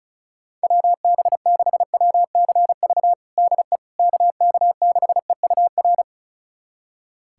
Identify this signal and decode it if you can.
Morse code, heard as a keyed tone, about 34 wpm, 700 Hz. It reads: WB6WCV DE KK6EUR
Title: USA Callsign DE Callsign - 6 Characters - 35wpm